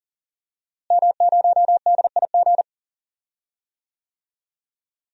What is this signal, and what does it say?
Morse code, heard as a keyed tone, about 40 wpm, 700 Hz. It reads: M0DIG
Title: DXCC-100 Call Signs - 5 Characters - Repeated 3x - 40wpm